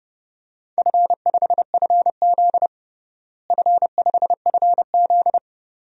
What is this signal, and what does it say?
Morse code, heard as a keyed tone, about 30 wpm, 700 Hz. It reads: F5FZ F5FZ
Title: DXCC-100 Call Signs - 4 Characters - Repeated 2x - 30wpm